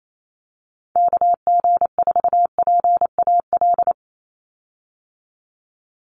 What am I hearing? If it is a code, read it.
Morse code, heard as a keyed tone, about 28 wpm, 700 Hz. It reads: KG4PAL
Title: USA Call Signs 28wpm